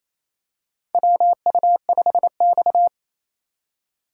Morse code, heard as a keyed tone, about 28 wpm, 700 Hz. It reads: WU5X
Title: USA Call Signs - 4 Characters - 4x Character Spacing - 28wpm